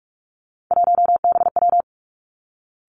Morse code, heard as a keyed tone, about 46 wpm, 700 Hz. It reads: 1BW
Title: Three Challenging Letter-Number Combinations 45wpm